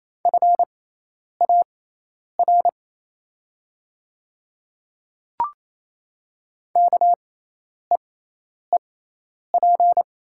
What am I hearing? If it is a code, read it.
Morse code, heard as a keyed tone, about 28 wpm, spaced as for 10 wpm, 690 Hz. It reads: FAR KEEP
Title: 6x Character Spacing - Top 500 Words - 28wpm